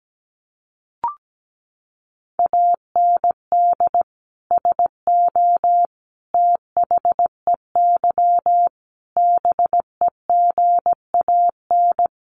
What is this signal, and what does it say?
Morse code, heard as a keyed tone, about 17 wpm, 700 Hz. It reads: AND SO THEY BEGAN
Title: Sentences from Top 400 Words 17wpm